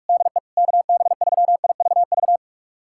Morse code, heard as a keyed tone, about 46 wpm, 700 Hz. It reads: DE KB3IVV
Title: USA Callsign DE Callsign - 6 Characters - 45wpm